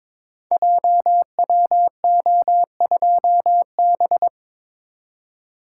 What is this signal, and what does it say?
Morse code, heard as a keyed tone, about 22 wpm, 700 Hz. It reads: JWO2B
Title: Code Groups - 5 Letter:Numbers - 22wpm